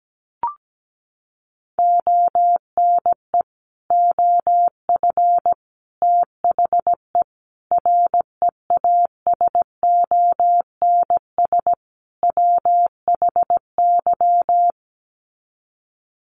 Morse code, heard as a keyed tone, about 17 wpm, 700 Hz. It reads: ONE OF THE REASONS WHY
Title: Sets of 5 Words 17wpm